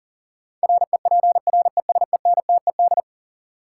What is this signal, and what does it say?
Morse code, heard as a keyed tone, about 40 wpm, 700 Hz. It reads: REPRESENTED
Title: Sets of 5 Words - New Word Drill - 40wpm